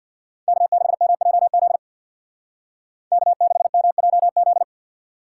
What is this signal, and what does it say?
Morse code, heard as a keyed tone, about 55 wpm, 700 Hz. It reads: K6MJZ K6MJZ
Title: Speed Racing - USA 1x3 Call Signs 50wpm